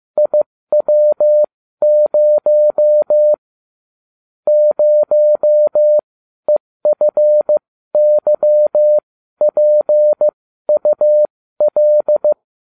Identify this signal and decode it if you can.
Morse code, heard as a keyed tone, about 15 wpm, spaced as for 12 wpm, 600 Hz. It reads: IW0 0EFYPUL